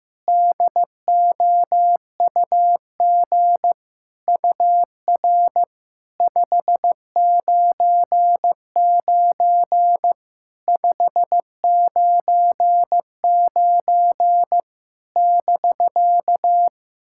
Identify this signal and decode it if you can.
Morse code, heard as a keyed tone, about 15 wpm, 700 Hz. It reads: DOUG UR 599 599 <BK>
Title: SOTA QSO Practice 15wpm